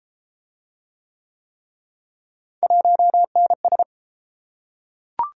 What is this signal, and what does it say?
Morse code, heard as a keyed tone, about 31 wpm, 700 Hz. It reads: 1NS